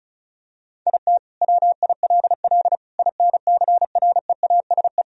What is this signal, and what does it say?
Morse code, heard as a keyed tone, about 34 wpm, 700 Hz. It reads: IT WILL INCREASE
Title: Sentences from Top 1000 Words - Review Part 4 - 35wpm